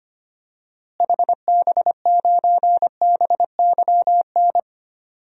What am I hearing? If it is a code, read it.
Morse code, heard as a keyed tone, about 25 wpm, 700 Hz. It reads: HB9BYN
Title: DXCC-100 Call Signs - 6 Characters - Repeated 2x - 25wpm